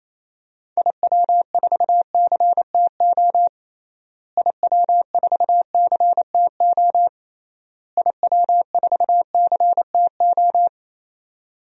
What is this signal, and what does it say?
Morse code, heard as a keyed tone, about 28 wpm, 700 Hz. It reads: IW4CTO IW4CTO IW4CTO
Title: DXCC-100 Call Signs - 6 Characters - Repeated 3x - 28wpm